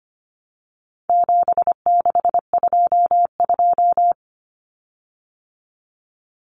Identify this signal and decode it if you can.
Morse code, heard as a keyed tone, about 25 wpm, 700 Hz. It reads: 7622
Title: Common Cut Numbers - 4-Digit 25wpm